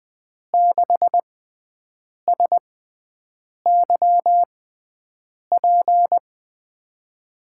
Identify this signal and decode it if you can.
Morse code, heard as a keyed tone, about 20 wpm, spaced as for 7 wpm, 700 Hz. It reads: 6SYP
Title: USA Call Signs - 5 Characters - 6x Character Spacing - 20wpm